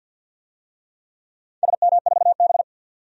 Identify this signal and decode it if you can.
Morse code, heard as a keyed tone, about 49 wpm, 700 Hz. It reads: SM4B
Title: DXCC-100 Call Signs - 4 Characters - 50wpm